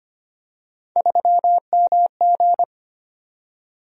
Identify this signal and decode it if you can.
Morse code, heard as a keyed tone, about 25 wpm, 700 Hz. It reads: 3MG
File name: Three Letter-Number Combinations - 25wpm